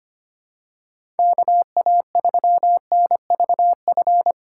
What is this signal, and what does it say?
Morse code, heard as a keyed tone, about 25 wpm, 700 Hz. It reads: KA3NVF
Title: USA Call Signs - 6 Characters - 3x Character Spacing - 25wpm